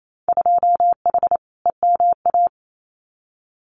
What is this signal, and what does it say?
Morse code, heard as a keyed tone, about 28 wpm, 700 Hz. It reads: 2H EMA
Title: Winter Field Day Exchanges 28wpm